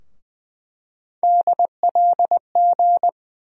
Morse code, heard as a keyed tone, about 20 wpm, 700 Hz. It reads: DLG
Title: USA Call Sign Postfixes - 3 Letters 20wpm